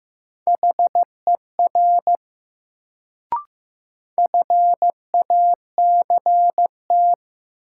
Morse code, heard as a keyed tone, about 15 wpm, 700 Hz. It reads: HER FACT
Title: Sets of 2 Words - New Word Drill - 15wpm